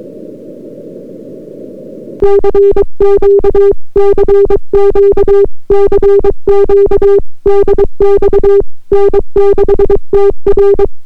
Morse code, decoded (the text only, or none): CQ CQ CQ DX N6TR